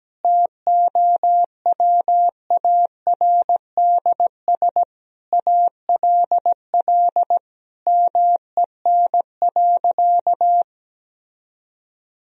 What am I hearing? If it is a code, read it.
Morse code, heard as a keyed tone, about 17 wpm, 700 Hz. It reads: TOWARDS ALL MEN.
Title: Sentences from Top 1000 Words - Review Part 2 - 17wpm